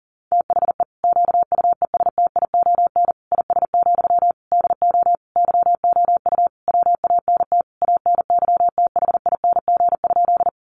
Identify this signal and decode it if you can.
Morse code, heard as a keyed tone, about 37 wpm, 700 Hz. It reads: THE QUESTION IS, DO YOU WANT ANYTHING?
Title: Sentences from Top 1000 Words 40wpm